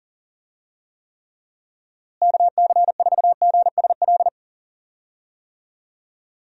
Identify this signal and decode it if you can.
Morse code, heard as a keyed tone, about 40 wpm, 700 Hz. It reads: KC4GSL
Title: USA Call Signs - 6 Characters - Repeated 2x - 40wpm